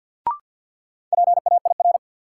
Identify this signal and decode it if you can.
Morse code, heard as a keyed tone, about 52 wpm, 700 Hz. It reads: PAIR
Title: Sets of 3 Words - New Word Drill - 50wpm